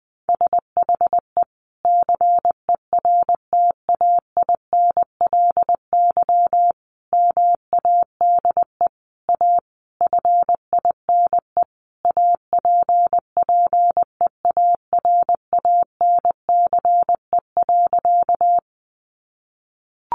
Morse code, heard as a keyed tone, about 20 wpm, 700 Hz. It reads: SHE CERTAINLY MADE A FINE APPEARANCE.
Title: Sentences from Top 1100 Words 20wpm